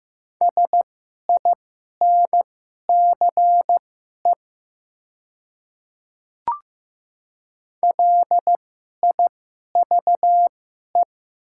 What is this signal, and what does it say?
Morse code, heard as a keyed tone, about 15 wpm, spaced as for 11 wpm, 700 Hz. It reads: SINCE LIVE
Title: 2x Character Spacing - Top 300 Words - 15wpm